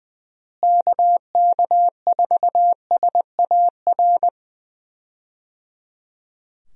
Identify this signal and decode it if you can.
Morse code, heard as a keyed tone, about 20 wpm, 700 Hz. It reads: KK4SAR
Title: USA Call Signs - 6 Characters - Repeated 3x - 20wpm